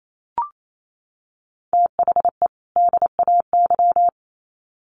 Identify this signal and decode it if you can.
Morse code, heard as a keyed tone, about 28 wpm, 700 Hz. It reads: THE DAY
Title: Sets of 2 Words 28wpm